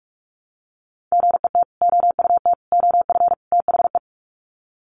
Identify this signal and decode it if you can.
Morse code, heard as a keyed tone, about 44 wpm, 700 Hz. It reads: GET OUT OF THE
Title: Sets of 4 Words 45wpm